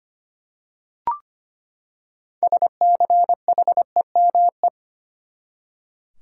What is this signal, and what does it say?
Morse code, heard as a keyed tone, about 25 wpm, 700 Hz. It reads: SCHEME
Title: Six Letter Words - 25wpm